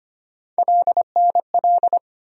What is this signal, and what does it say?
Morse code, heard as a keyed tone, about 25 wpm, 700 Hz. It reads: LNL
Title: USA Call Sign Postfixes - 3 Letters 25wpm